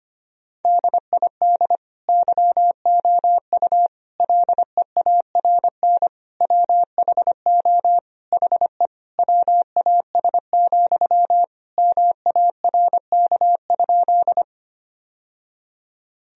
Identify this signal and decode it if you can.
Morse code, heard as a keyed tone, about 25 wpm, 700 Hz. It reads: DID YOU LEARN WHO HE WAS, MARK?
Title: Sentences from Top 1000 Words - Review Part 1 - 25wpm